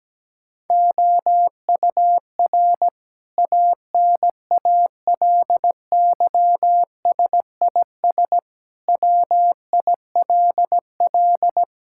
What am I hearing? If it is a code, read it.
Morse code, heard as a keyed tone, about 17 wpm, 700 Hz. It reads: OUR ANALYSIS WILL
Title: Sentences from Top 1000 Words - Review Part 5 - 17wpm